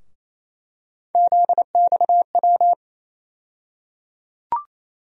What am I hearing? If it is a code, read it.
Morse code, heard as a keyed tone, about 28 wpm, 700 Hz. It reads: ZXW
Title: USA Call Sign Postfixes - 3 Letters 28wpm